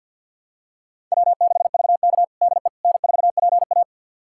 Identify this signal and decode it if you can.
Morse code, heard as a keyed tone, about 46 wpm, 700 Hz. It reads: W6VK DE N4PA
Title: USA Callsign DE Callsign - 4 Characters - 50wpm